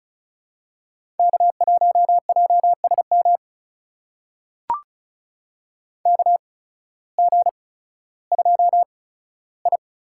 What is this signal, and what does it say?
Morse code, heard as a keyed tone, about 34 wpm, spaced as for 16 wpm, 700 Hz. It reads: K1JSM KG2I
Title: USA Call Signs - 8x Character Spacing - 35wpm